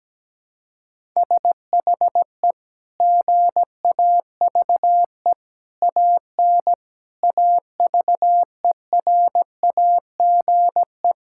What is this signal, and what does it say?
Morse code, heard as a keyed tone, about 17 wpm, 700 Hz. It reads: SHE GAVE AN AVERAGE
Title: Sentences from Top 1000 Words - Review Part 6 - 17wpm